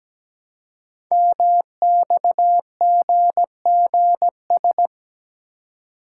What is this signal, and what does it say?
Morse code, heard as a keyed tone, about 17 wpm, 700 Hz. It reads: MXGGS
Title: Code Groups - 5 Letters - 17wpm